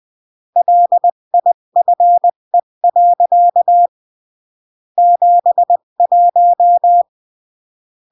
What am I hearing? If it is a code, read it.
Morse code, heard as a keyed tone, about 20 wpm, 700 Hz. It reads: LIFE. 71
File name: CW / jodorowsky / 82 commandments 20-15